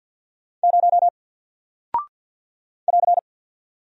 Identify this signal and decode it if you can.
Morse code, heard as a keyed tone, about 48 wpm, 700 Hz. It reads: 0 <AR>
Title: Single Character (Letter:Number:Punctuation:Prosign) Alt 50wpm